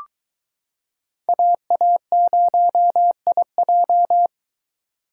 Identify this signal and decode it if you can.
Morse code, heard as a keyed tone, about 22 wpm, 700 Hz. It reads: AA0IJ